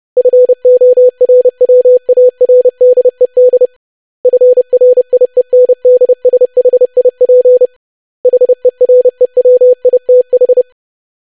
Morse code, heard as a keyed tone, about 30 wpm, 500 Hz. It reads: FORWARDED FRIENDSHIP HEREWITH
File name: lesson / web / longwords-25wpm